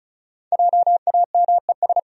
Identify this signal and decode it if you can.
Morse code, heard as a keyed tone, about 34 wpm, 700 Hz. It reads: JAMES
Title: USA Common Names - Max 5 Letters - 35wpm